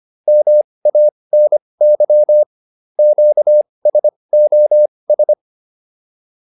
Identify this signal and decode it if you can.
Morse code, heard as a keyed tone, about 25 wpm, spaced as for 20 wpm, 600 Hz. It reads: MANY QSOS
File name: PR303 20